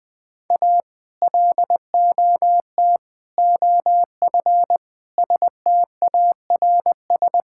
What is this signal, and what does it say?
Morse code, heard as a keyed tone, about 20 wpm, 700 Hz. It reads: A LOT OF STARS
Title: Sentences with 2-Word Phrases - Spoken Beginning - 20wpm